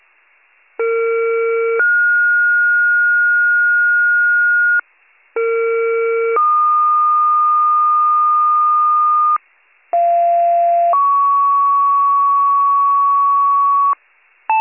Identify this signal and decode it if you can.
Two-tone page, heard as a one-second tone, then a three-second tone, about 460 then 1430 Hz, about 460 then 1220 Hz, about 690 then 1120 Hz. It is heard from 0.8 to 13.9 s.